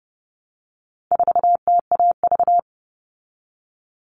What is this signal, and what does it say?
Morse code, heard as a keyed tone, about 30 wpm, 700 Hz. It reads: 4TAV
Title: All Cut Numbers - 4-Digit 30wpm